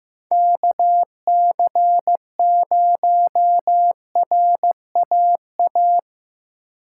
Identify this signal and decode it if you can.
Morse code, heard as a keyed tone, about 17 wpm, 700 Hz. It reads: KC0RAA
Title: Speed Racing - USA 2x3 Call Signs 15wpm